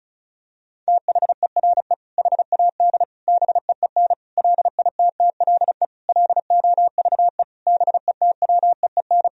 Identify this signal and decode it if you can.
Morse code, heard as a keyed tone, about 34 wpm, 700 Hz. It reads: THERE HAD BEEN LITTLE LOVE BETWEEN
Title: Sentences from Top 1000 Words - Review Part 1 - 35wpm